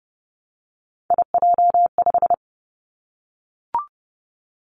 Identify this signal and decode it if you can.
Morse code, heard as a keyed tone, about 30 wpm, 700 Hz. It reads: IJ5
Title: Three Challenging Letter-Number Combinations 30wpm